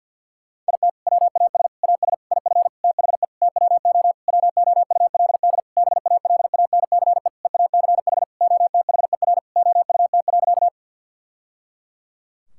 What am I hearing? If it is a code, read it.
Morse code, heard as a keyed tone, about 52 wpm, 700 Hz. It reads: IT WAS AS IF THE TWO WOULD BALANCE EACH OTHER OUT.
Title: Sentences from Top 1000 Words - Review Part 6 - 50wpm